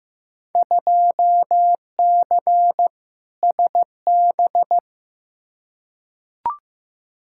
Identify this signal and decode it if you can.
Morse code, heard as a keyed tone, about 15 wpm, 700 Hz. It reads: 2C SB
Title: Field Day Exchanges Encore - 15wpm